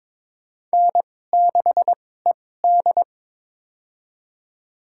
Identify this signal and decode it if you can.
Morse code, heard as a keyed tone, about 21 wpm, spaced as for 16 wpm, 700 Hz. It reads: N6ED
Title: USA Call Signs - 4 Characters - 2x Character Spacing - 22wpm